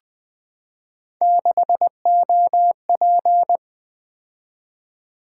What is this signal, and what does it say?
Morse code, heard as a keyed tone, about 20 wpm, 700 Hz. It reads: 6OP
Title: Three Letter-Number Combinations - 20wpm